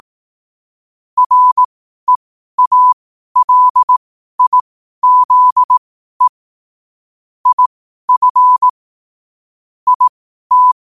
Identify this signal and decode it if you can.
Morse code, heard as a keyed tone, about 18 wpm, spaced as for 12 wpm, 1000 Hz. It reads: REALIZE IF IT
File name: 12wpm